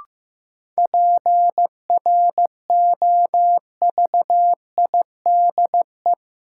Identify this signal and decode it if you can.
Morse code, heard as a keyed tone, about 15 wpm, 700 Hz. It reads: PROVIDE
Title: Sentences from Top 1000 Words - Review Part 2 - New Word Drill - 15wpm